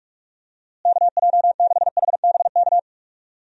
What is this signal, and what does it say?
Morse code, heard as a keyed tone, about 46 wpm, 700 Hz. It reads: KJ6HBK
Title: USA Call Signs - 6 Characters - Repeated 3x - 45wpm